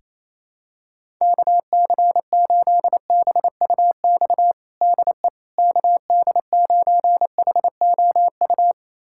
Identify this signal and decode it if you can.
Morse code, heard as a keyed tone, about 28 wpm, 700 Hz. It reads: KC8BUX DE KD9HOU
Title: USA Callsign DE Callsign - 6 Characters - 28wpm